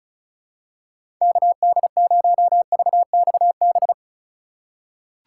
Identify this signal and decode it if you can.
Morse code, heard as a keyed tone, about 34 wpm, 700 Hz. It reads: KD0VXB
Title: USA Call Signs - 6 Characters - 4x Character Spacing - 35wpm